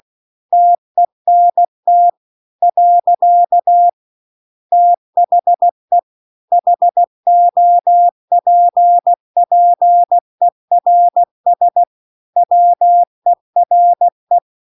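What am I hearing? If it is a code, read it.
Morse code, heard as a keyed tone, about 16 wpm, 700 Hz. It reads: TENT . THE HOPPERS WERE
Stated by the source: Morse practice text at 16 WPM.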